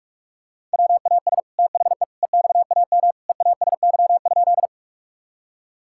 Morse code, heard as a keyed tone, about 46 wpm, 700 Hz. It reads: WAS THE EXAM EASY?
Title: Sentences from Top 1000 Words - Review Part 5 - 45wpm